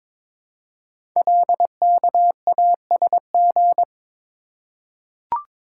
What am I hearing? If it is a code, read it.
Morse code, heard as a keyed tone, about 22 wpm, 700 Hz. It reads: LKASG
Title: Code Groups - 5 Letters - 22wpm